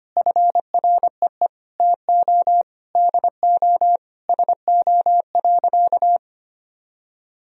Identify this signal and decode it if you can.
Morse code, heard as a keyed tone, about 25 wpm, 700 Hz. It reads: FREE TO DO SO.
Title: Sentences from Top 500 Words and Abbreviations 25wpm